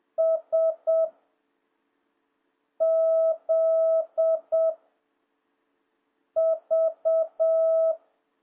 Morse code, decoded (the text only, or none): S Z V